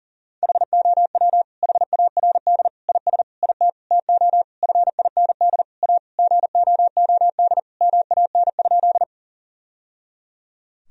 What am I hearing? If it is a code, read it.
Morse code, heard as a keyed tone, about 40 wpm, 700 Hz. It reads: HOW HARD IS IT TO FIND A GOOD MAN?
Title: Sentences from Top 1000 Words - Review Part 1 - 40wpm